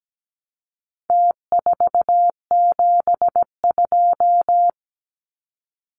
Morse code, heard as a keyed tone, about 17 wpm, 700 Hz. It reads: T472
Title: Numbers - Contest Style - 4-Digit 17wpm